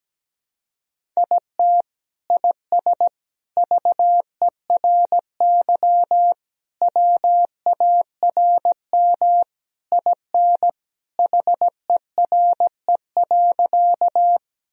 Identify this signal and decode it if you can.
Morse code, heard as a keyed tone, about 17 wpm, 700 Hz. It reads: IT IS VERY WARM IN HERE.
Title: Sentences from Top 1000 Words - Review Part 2 - 17wpm